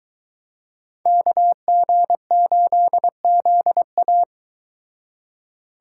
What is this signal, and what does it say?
Morse code, heard as a keyed tone, about 23 wpm, 700 Hz. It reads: KG8ZA